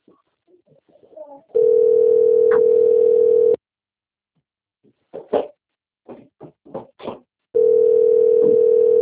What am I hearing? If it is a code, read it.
Ringback tone (North America), 440 + 480 Hz.